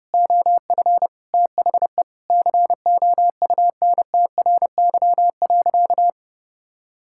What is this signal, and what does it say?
Morse code, heard as a keyed tone, about 30 wpm, 700 Hz. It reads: OF THE COUNTRY.
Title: Sentences from Top 1000 Words - Review Part 6 - 30wpm